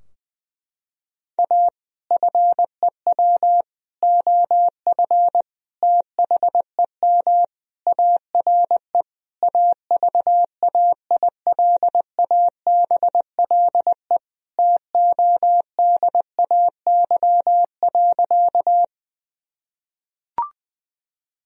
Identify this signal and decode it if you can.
Morse code, heard as a keyed tone, about 20 wpm, 700 Hz. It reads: A FEW OF THEM ARE AVAILABLE TODAY.
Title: Sentences from Top 1000 Words - Review Part 3 - 20wpm